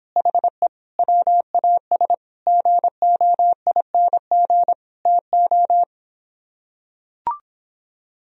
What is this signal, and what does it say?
Morse code, heard as a keyed tone, about 26 wpm, 700 Hz. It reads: HE WAS GOING TO